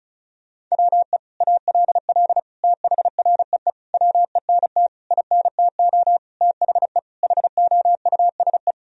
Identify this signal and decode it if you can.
Morse code, heard as a keyed tone, about 36 wpm, 700 Hz. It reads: WE ALL THREE WENT INTO THE HOUSE
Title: Sentences from Top 1000 Words - Review Part 1 - 35wpm